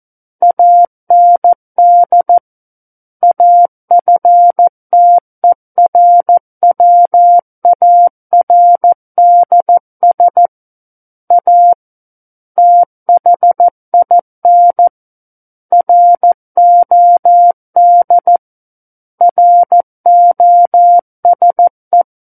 Morse code, heard as a keyed tone, about 14 wpm, 700 Hz. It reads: AND AFTERWARDS A THIN ROD ROSE
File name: Book One / 05-Chapter 5 - 14 WPM